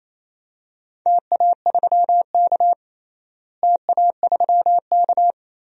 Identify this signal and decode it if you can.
Morse code, heard as a keyed tone, about 28 wpm, 700 Hz. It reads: TA3K TA3K
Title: DXCC-100 Call Signs - 4 Characters - Repeated 3x - 28wpm